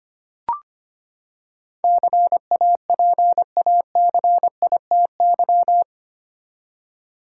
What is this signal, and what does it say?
Morse code, heard as a keyed tone, about 25 wpm, 700 Hz. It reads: CAPACITY
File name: Eight Letter Words - 25wpm